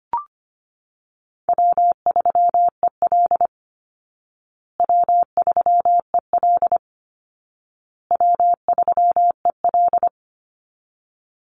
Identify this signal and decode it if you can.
Morse code, heard as a keyed tone, about 25 wpm, 700 Hz. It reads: W3EL W3EL W3EL
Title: Call Signs Repeated 2x - 25wpm